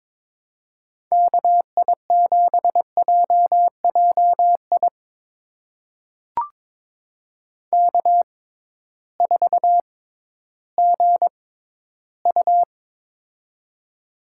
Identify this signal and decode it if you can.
Morse code, heard as a keyed tone, about 22 wpm, spaced as for 13 wpm, 700 Hz. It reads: KI7JJI K4GU
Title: USA Call Signs - 6x Character Spacing - 22wpm